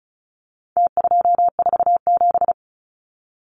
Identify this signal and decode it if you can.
Morse code, heard as a keyed tone, about 34 wpm, 700 Hz. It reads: T247
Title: Numbers - Contest Style - 4-Digit 35wpm